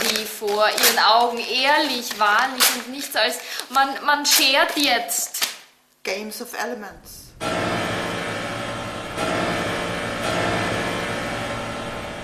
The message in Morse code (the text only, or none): none